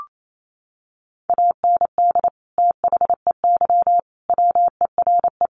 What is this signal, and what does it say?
Morse code, heard as a keyed tone, about 28 wpm, 700 Hz. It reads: AND THEY WERE
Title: Sentences from Top 1500 Words 28wpm